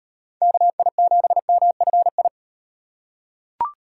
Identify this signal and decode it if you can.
Morse code, heard as a keyed tone, about 36 wpm, 700 Hz. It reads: KI7MFI